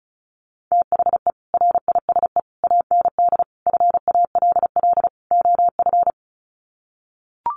Morse code, heard as a keyed tone, about 34 wpm, 700 Hz. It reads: THE RISE AND FALL OF
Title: Sets of 5 Words Encore2 35wpm